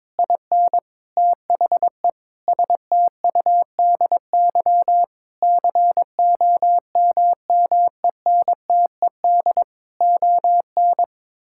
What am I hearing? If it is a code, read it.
Morse code, heard as a keyed tone, about 22 wpm, 700 Hz. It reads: IN THE STUDY COMMENTED ON
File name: Sentences with 7-Word Phrases - 22wpm